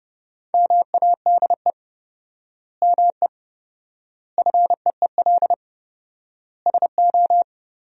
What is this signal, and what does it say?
Morse code, heard as a keyed tone, about 32 wpm, 700 Hz. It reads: MADE ME FEEL SO
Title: 4x Word-Spacing - Sentences from Top 400 Words - 30wpm